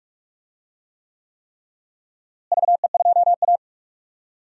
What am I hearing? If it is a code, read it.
Morse code, heard as a keyed tone, about 42 wpm, 700 Hz. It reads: VE2A
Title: DXCC-100 Call Signs - 4 Characters - Repeated 3x - 45wpm